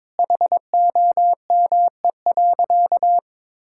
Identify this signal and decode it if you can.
Morse code, heard as a keyed tone, about 21 wpm, 700 Hz. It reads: HOME.
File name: practice / 22wpm / Sentences with 4-Word Phrases - 22wpm